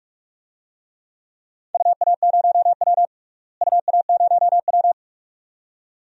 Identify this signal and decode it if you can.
Morse code, heard as a keyed tone, about 46 wpm, 700 Hz. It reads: UA0W UA0W
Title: DXCC-100 Call Signs - 4 Characters - Repeated 2x - 45wpm